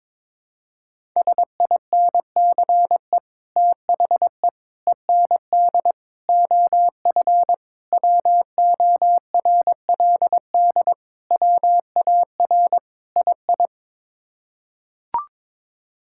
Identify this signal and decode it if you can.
Morse code, heard as a keyed tone, about 22 wpm, 700 Hz. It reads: SINCE THE END OF WORLD WAR II
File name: practice / 22wpm / Sets of 7 Words - 22wpm